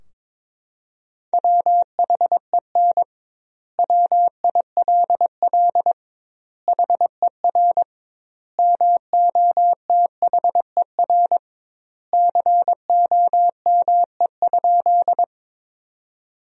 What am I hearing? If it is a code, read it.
Morse code, heard as a keyed tone, about 22 wpm, 700 Hz. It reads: WHEN WILL HER MOTHER COME?
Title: 2x Word-Spacing - Sentences from Top 200 Words - 22wpm